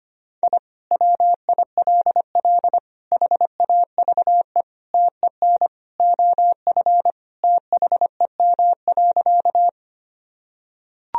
Morse code, heard as a keyed tone, about 25 wpm, 700 Hz. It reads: I WILL HAVE TEN OF THEM.
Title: Sentences from Top 1000 Words - Review Part 2 - 25wpm